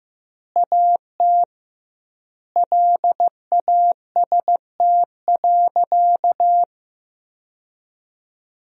Morse code, heard as a keyed tone, about 15 wpm, 700 Hz. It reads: AT LAST.
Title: 2x Word-Spacing - Sentences from Top 300 Words - 15wpm